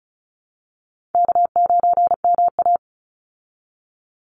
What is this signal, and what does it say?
Morse code, heard as a keyed tone, about 34 wpm, 700 Hz. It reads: K9MA
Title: Easy Call Signs Repeated 2x - 35wpm